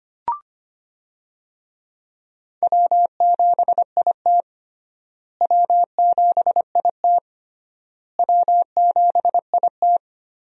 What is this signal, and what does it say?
Morse code, heard as a keyed tone, about 25 wpm, 700 Hz. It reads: W7IT W7IT W7IT
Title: USA Call Signs - 4 Characters - Repeated 3x - 25wpm